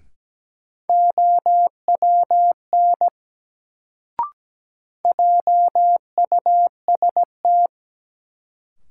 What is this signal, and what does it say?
Morse code, heard as a keyed tone, about 17 wpm, 700 Hz. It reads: OWN JUST
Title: Sets of 2 Words - New Word Drill - 17wpm